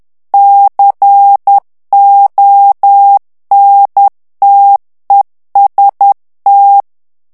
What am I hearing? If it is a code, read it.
Morse code, heard as a keyed tone, about 10 wpm, 800 Hz. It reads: CONTEST
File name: ITALIEN texte clair 2 vitesse 10 mots minute